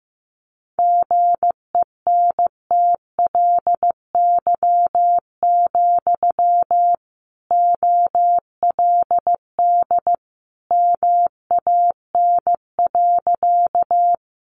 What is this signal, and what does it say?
Morse code, heard as a keyed tone, about 14 wpm, 700 Hz. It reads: GENTLY, OLD MAN.
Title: Sentences from Top 1500 Words 15wpm